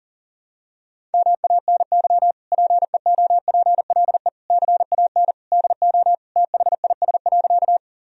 Morse code, heard as a keyed tone, about 37 wpm, 700 Hz. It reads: MANY PEOPLE CAN DO THIS.
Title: Sentences from Top 1000 Words - Review Part 1 - 40wpm